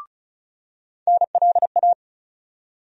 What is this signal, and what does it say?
Morse code, heard as a keyed tone, about 34 wpm, 700 Hz. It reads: NLA
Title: USA Call Sign Postfixes - 3 Letters 35wpm